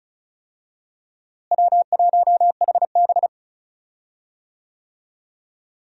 Morse code, heard as a keyed tone, about 34 wpm, 700 Hz. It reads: W1HB
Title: USA Call Signs - 4 Characters - Repeated 3x - 35wpm